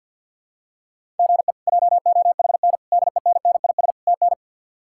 Morse code, heard as a keyed tone, about 49 wpm, 700 Hz. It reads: GE JOHN DENNIS TN
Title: K1USN SST QSO Practice - 50wpm